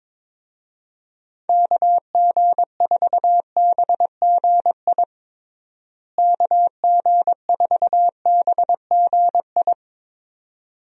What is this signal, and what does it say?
Morse code, heard as a keyed tone, about 22 wpm, 700 Hz. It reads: KG4BGI KG4BGI
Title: USA Call Signs - 6 Characters - Repeated 2x - 22wpm